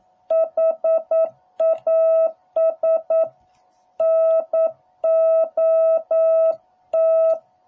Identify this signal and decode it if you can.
Morse code, heard as a keyed tone, about 9 wpm, 660 Hz. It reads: HAS NOT